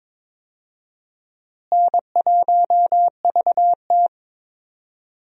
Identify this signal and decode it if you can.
Morse code, heard as a keyed tone, about 22 wpm, 700 Hz. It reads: N1VT
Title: USA Call Signs - 4 Characters - Repeated 2x - 22wpm